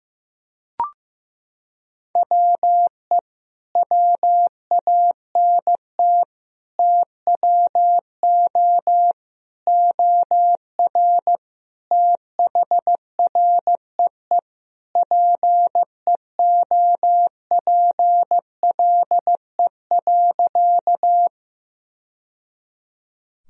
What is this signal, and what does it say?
Morse code, heard as a keyed tone, about 15 wpm, 700 Hz. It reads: WE WANT TWO OR THREE PEOPLE.
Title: Sentences from Top 1000 Words - Review Part 1 - 15wpm